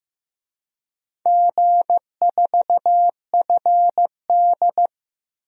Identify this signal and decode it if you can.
Morse code, heard as a keyed tone, about 15 wpm, 700 Hz. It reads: G4FD
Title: DXCC-100 Call Signs - 4 Characters - 15wpm